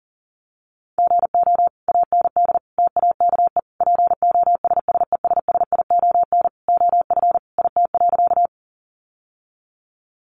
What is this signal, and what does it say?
Morse code, heard as a keyed tone, about 37 wpm, 700 Hz. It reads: GO AND TAKE POSSESSION OF IT.
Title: Sentences from Top 1400 Words 40wpm